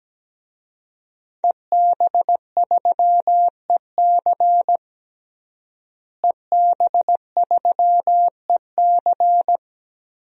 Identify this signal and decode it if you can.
Morse code, heard as a keyed tone, about 17 wpm, 700 Hz. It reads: EB3EC EB3EC
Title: DXCC-100 Call Signs - 5 Characters - Repeated 2x - 17wpm